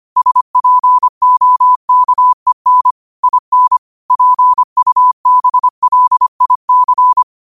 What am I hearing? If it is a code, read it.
Morse code, heard as a keyed tone, about 25 wpm, 1000 Hz. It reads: SPOKEN IN PUBLIC
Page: New quotes every day in morse code at 25 Words per minute.